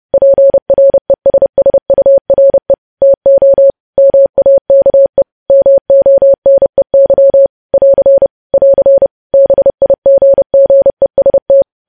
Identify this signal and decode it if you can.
Morse code, heard as a keyed tone, about 30 wpm, 550 Hz. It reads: PRESSURE TO MAKE MONEY <AR> <AR> BIGGEST
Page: news headlines in morse code at 30 wpm